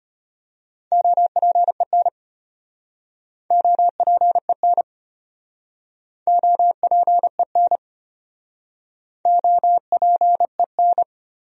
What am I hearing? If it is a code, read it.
Morse code, heard as a keyed tone, about 33 wpm, 700 Hz. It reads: OPEN OPEN OPEN OPEN